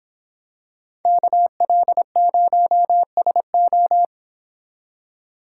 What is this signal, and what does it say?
Morse code, heard as a keyed tone, about 25 wpm, 700 Hz. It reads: KL0SO